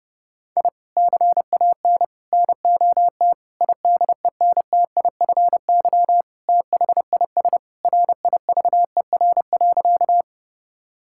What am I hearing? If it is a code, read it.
Morse code, heard as a keyed tone, about 32 wpm, 700 Hz. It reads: I CAN NOT IDENTIFY THIS RIVER.
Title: Sentences from Top 1000 Words - Review Part 6 - 30wpm